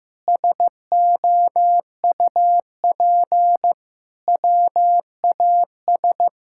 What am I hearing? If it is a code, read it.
Morse code, heard as a keyed tone, about 16 wpm, 700 Hz. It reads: SOUP WAS
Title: Sentences from Top 1000 Words - Review Part 6 - 15wpm